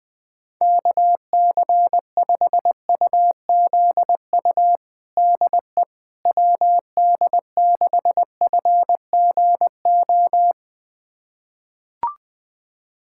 Morse code, heard as a keyed tone, about 20 wpm, 700 Hz. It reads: KC5UZU DE WD6FGO
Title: USA Callsign DE Callsign 20wpm